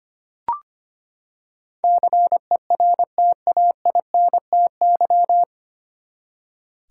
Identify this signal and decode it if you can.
Morse code, heard as a keyed tone, about 25 wpm, 700 Hz. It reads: CERTAINTY
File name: Nine Letter Words - 25wpm